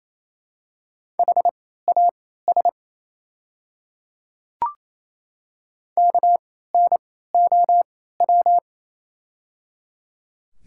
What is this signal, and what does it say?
Morse code, heard as a keyed tone, about 27 wpm, spaced as for 16 wpm, 700 Hz. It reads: HAS KNOW
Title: 3x Character Spacing - Top 100 Words - 28wpm